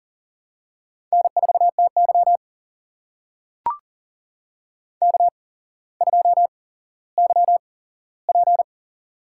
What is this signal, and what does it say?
Morse code, heard as a keyed tone, about 40 wpm, spaced as for 17 wpm, 700 Hz. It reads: N4TY K2YP
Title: USA Call Signs - 4 Characters - 8x Character Spacing - 40wpm